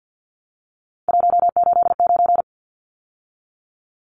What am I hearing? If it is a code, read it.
Morse code, heard as a keyed tone, about 53 wpm, 700 Hz. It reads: 189
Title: Numbers - 3-Digit 50wpm